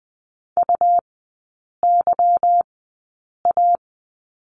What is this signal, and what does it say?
Morse code, heard as a keyed tone, about 20 wpm, spaced as for 8 wpm, 700 Hz. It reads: UYA
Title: 205 Lesson 33e - Letter j - Three Characters 20wpm